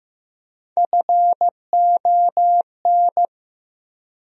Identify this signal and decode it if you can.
Morse code, heard as a keyed tone, about 15 wpm, 700 Hz. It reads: FON
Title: USA Call Sign Postfixes - 3 Letters 15wpm